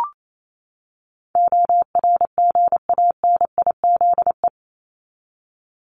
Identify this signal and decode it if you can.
Morse code, heard as a keyed tone, about 28 wpm, 700 Hz. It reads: ORGANIZE
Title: Top 1751-2000 Words 28wpm